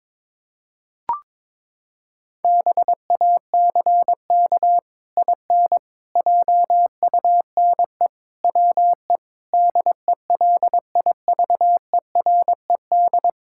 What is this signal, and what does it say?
Morse code, heard as a keyed tone, about 22 wpm, 700 Hz. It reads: BACK IN JUNE WE DELIVERED
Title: Pangram Challenge v2 - 22wpm